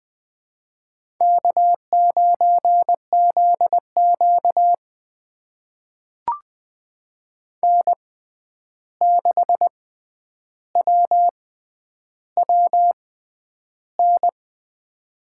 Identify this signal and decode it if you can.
Morse code, heard as a keyed tone, about 21 wpm, spaced as for 10 wpm, 700 Hz. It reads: K9ZQ N6WWN
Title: USA Call Signs - 6x Character Spacing - 20wpm